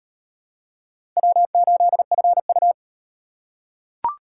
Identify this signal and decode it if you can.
Morse code, heard as a keyed tone, about 38 wpm, 700 Hz. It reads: W8FU